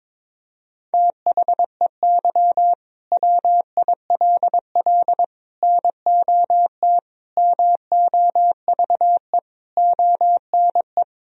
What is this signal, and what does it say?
Morse code, heard as a keyed tone, about 22 wpm, 700 Hz. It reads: THEY WILL NOT MOVE ONE
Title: Sentences from Top 1000 Words - Review Part 3 - 22wpm